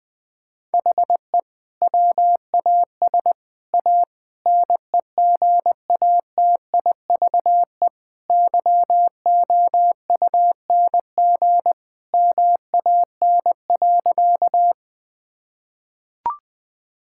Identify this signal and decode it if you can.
Morse code, heard as a keyed tone, about 20 wpm, 700 Hz. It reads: HE WAS A NEGATIVE YOUNG MAN.
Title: Sentences from Top 1000 Words - Review Part 6 - 20wpm